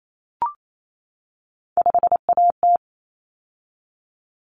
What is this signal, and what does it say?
Morse code, heard as a keyed tone, about 28 wpm, 700 Hz. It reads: 5AT
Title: Common Cut Numbers - 3-Digit 28wpm